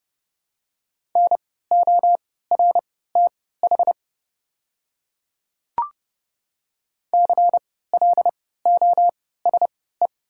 Morse code, heard as a keyed tone, about 31 wpm, spaced as for 17 wpm, 700 Hz. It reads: NORTH CLOSE
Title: 3x Character Spacing - Top 300 Words - 30wpm